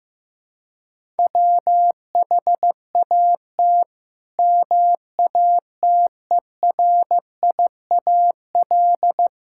Morse code, heard as a keyed tone, about 15 wpm, 700 Hz. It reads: WHAT MATERIAL
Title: Sentences from Top 1000 Words - Review Part 5 - 15wpm